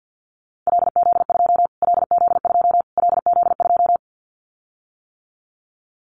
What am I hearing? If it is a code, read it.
Morse code, heard as a keyed tone, about 50 wpm, 700 Hz. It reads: LZ2 LZ2 LZ2
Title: Callsign Prefixes for Top 25 DXCC - Repeated 3x - 50wpm